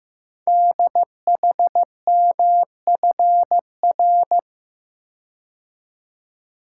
Morse code, heard as a keyed tone, about 15 wpm, 690 Hz. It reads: DHMFR
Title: Code Groups - 5 Letters - 15wpm